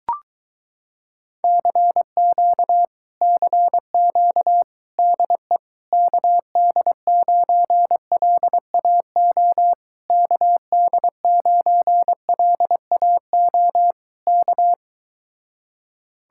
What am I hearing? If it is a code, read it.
Morse code, heard as a keyed tone, about 22 wpm, 700 Hz. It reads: CQ CQ DE KD9LAO KD9LAO K